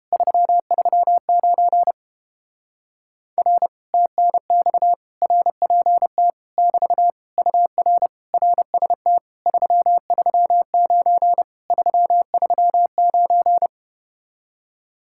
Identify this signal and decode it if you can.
Morse code, heard as a keyed tone, about 32 wpm, 700 Hz. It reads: 339 R TNX RPT <BT> UR RST 339 339